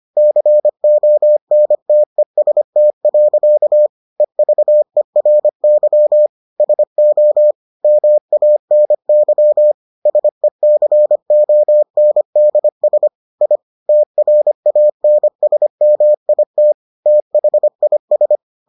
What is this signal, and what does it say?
Morse code, heard as a keyed tone, about 25 wpm, 600 Hz. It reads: CONTEST. EVERY SO MANY SECONDS I TRANSMIT THIS